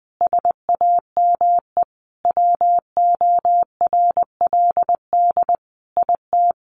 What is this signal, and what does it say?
Morse code, heard as a keyed tone, about 20 wpm, 700 Hz. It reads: SAME WORLD IT
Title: Sentences from Top 1300 Words 20wpm